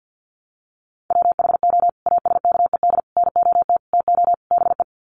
Morse code, heard as a keyed tone, about 50 wpm, 700 Hz. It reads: WHO ASKED NOT TO BE
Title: Sets of 5 Words Encore2 50wpm